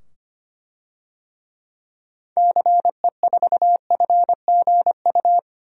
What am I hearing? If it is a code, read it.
Morse code, heard as a keyed tone, about 25 wpm, 700 Hz. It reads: CE4FGU
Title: DXCC-100 Call Signs - 6 Characters - 25wpm